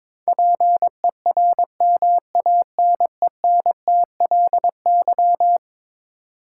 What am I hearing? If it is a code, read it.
Morse code, heard as a keyed tone, about 22 wpm, 700 Hz. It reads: PERMANENTLY
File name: practice / 22wpm / Eleven Letter Words - 22wpm